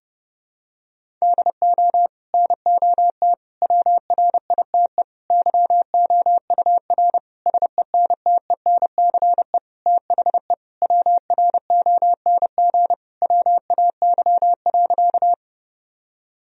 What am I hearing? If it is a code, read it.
Morse code, heard as a keyed tone, about 32 wpm, 700 Hz. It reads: DO NOT WRITE YOUR SENTENCE THE WRONG WAY.
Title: Sentences from Top 1000 Words - Review Part 1 - 30wpm